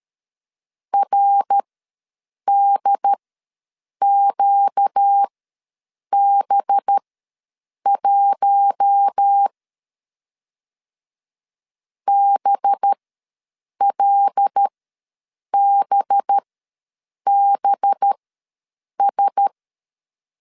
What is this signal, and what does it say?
Morse code, heard as a keyed tone, about 13 wpm, spaced as for 7 wpm, 800 Hz. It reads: RDQB1 BLBBS
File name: 5 LB ws 12wpm